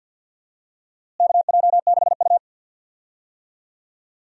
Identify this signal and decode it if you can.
Morse code, heard as a keyed tone, about 47 wpm, 700 Hz. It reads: KJ6U
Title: USA Call Signs - 4 Characters - Repeated 3x - 50wpm